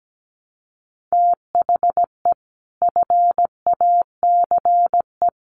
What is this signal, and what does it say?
Morse code, heard as a keyed tone, about 17 wpm, 700 Hz. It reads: THE FACE
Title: Sets of 5 Words Encore 17wpm